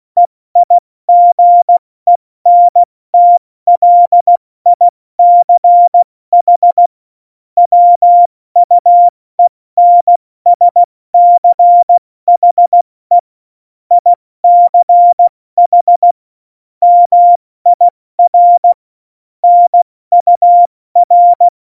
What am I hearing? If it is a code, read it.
Morse code, heard as a keyed tone, about 16 wpm, spaced as for 14 wpm, 700 Hz. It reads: EIGENTLICH WUENSCHE ICH MIR NUR
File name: Morsekurs / Witze / Witz 07. Januar 2020, 70 F80